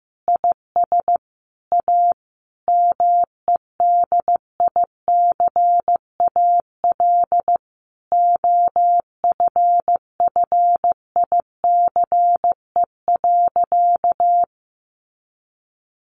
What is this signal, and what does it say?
Morse code, heard as a keyed tone, about 15 wpm, 700 Hz. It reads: IS A MEDICAL OFFICE.
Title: Additional Easy Phrases 15wpm